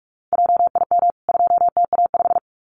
Morse code, heard as a keyed tone, about 44 wpm, 700 Hz. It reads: JIM 2TA5
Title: CWT Exchange 45wpm